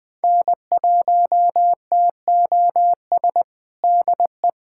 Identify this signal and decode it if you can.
Morse code, heard as a keyed tone, about 21 wpm, 700 Hz. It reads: N1TOS DE
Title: USA Callsign DE Callsign - 5 Characters - 20wpm